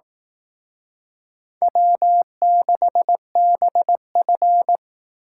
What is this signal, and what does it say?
Morse code, heard as a keyed tone, about 18 wpm, 700 Hz. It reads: W6BF
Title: Speed Racing - USA 1x2 Call Signs 15wpm